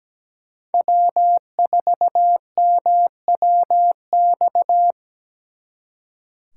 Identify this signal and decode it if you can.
Morse code, heard as a keyed tone, about 20 wpm, 700 Hz. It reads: W4MWX
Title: Speed Racing - USA 1x3 Call Signs 17wpm